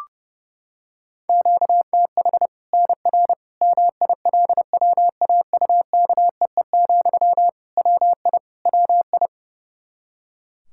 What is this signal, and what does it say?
Morse code, heard as a keyed tone, about 30 wpm, 700 Hz. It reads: QTH NR MILWAUKEE, WI WI
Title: K1USN SST QSO Practice 30wpm